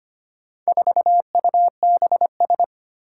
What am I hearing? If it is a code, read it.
Morse code, heard as a keyed tone, about 25 wpm, 700 Hz. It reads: 4UBS
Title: Four Challenging Letter-Number Combinations 25wpm